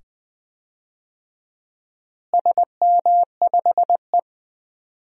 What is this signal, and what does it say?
Morse code, heard as a keyed tone, about 21 wpm, 700 Hz. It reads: SM5E
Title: DXCC-100 Call Signs - 4 Characters - Repeated 2x - 20wpm